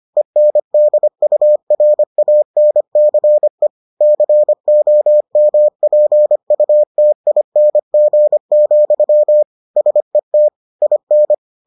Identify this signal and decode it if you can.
Morse code, heard as a keyed tone, about 25 wpm, 600 Hz. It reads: ENDURANCE COMPUTING, SET IN